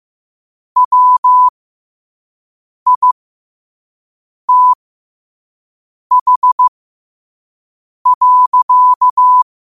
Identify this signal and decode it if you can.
Morse code, heard as a keyed tone, about 15 wpm, spaced as for 5 wpm, 1000 Hz. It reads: WITH.
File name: qotdfiles / 5wpm